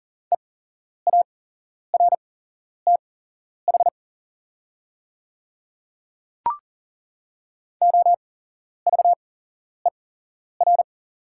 Morse code, heard as a keyed tone, about 40 wpm, spaced as for 11 wpm, 690 Hz. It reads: EARTH OVER
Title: 8x Character Spacing - Top 500 Words - 40wpm